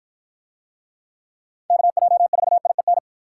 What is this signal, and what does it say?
Morse code, heard as a keyed tone, about 48 wpm, 700 Hz. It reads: KJ4IEN